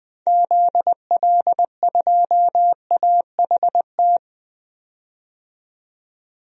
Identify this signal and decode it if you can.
Morse code, heard as a keyed tone, about 20 wpm, 700 Hz. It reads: ZL2AHT
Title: DXCC-100 Call Signs - 6 Characters - 20wpm